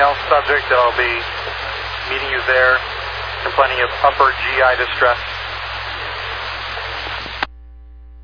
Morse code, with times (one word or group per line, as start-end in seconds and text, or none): none